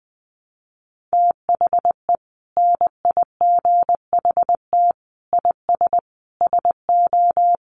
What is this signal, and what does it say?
Morse code, heard as a keyed tone, about 20 wpm, 700 Hz. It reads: THE NIGHT IS SO
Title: Sentences from Top 500 Words 20wpm